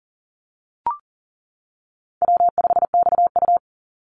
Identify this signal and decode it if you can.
Morse code, heard as a keyed tone, about 40 wpm, 700 Hz. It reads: W5XU
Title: Easy Call Signs - 40wpm